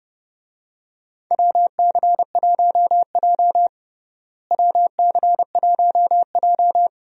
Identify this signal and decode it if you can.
Morse code, heard as a keyed tone, about 31 wpm, 700 Hz. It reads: WC1J WC1J
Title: USA Call Signs - 4 Characters - Repeated 3x - 30wpm